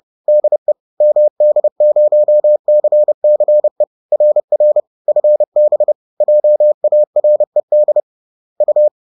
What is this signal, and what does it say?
Morse code, heard as a keyed tone, about 30 wpm, 600 Hz. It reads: DE MD0CCE RR FB JARED U